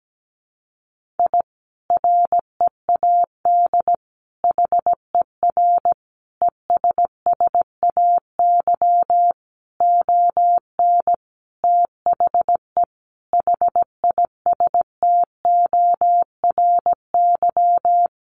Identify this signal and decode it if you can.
Morse code, heard as a keyed tone, about 17 wpm, 700 Hz. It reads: I READ HER ESSAY ON THE HISTORY
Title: Sentences from Top 1500 Words 17wpm